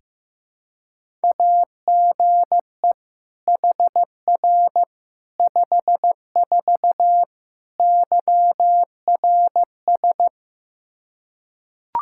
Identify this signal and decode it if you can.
Morse code, heard as a keyed tone, about 15 wpm, 700 Hz. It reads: AGE HR 54 YRS
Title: Practice QSO - Exchange 3 - More Info - 15wpm